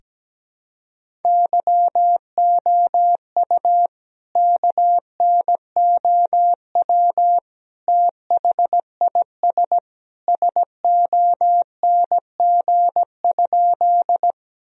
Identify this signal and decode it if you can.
Morse code, heard as a keyed tone, about 17 wpm, 700 Hz. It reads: YOU KNOW THIS SONG?
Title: Sentences from Top 1000 Words - Review Part 2 - 17wpm